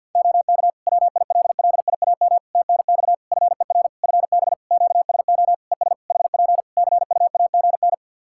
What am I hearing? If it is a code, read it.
Morse code, heard as a keyed tone, about 48 wpm, 700 Hz. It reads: OK WILLIAM TNX FER FB QSO ES HP CUAGN
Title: Practice QSO - Exchange 4 - Ending - 50wpm